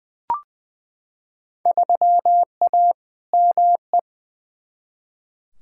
Morse code, heard as a keyed tone, about 20 wpm, 700 Hz. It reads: 3A ME
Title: Field Day Exchanges - 20wpm